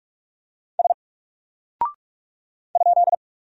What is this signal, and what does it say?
Morse code, heard as a keyed tone, about 44 wpm, 700 Hz. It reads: S ?
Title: Single Character (Letter:Number:Punctuation:Prosign) Alt 45wpm